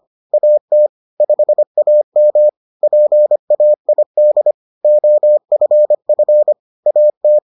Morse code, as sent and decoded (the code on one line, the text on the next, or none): .- - / ..... .- -- / .--. .- .. -.. / --- ..-. ..-. / .- -
AT 5AM PAID OFF AT